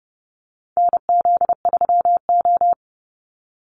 Morse code, heard as a keyed tone, about 30 wpm, 700 Hz. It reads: NZ3O
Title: Call Signs - 30wpm